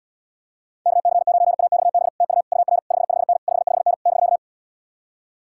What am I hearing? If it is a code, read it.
Morse code, heard as a keyed tone, about 79 wpm, 700 Hz. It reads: KC9AXB TU UR 55N 55N <BK>